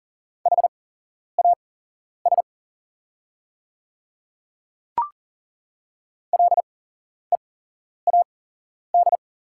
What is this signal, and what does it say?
Morse code, heard as a keyed tone, about 40 wpm, spaced as for 11 wpm, 690 Hz. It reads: HAS LEAD
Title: 8x Character Spacing - Top 500 Words - 40wpm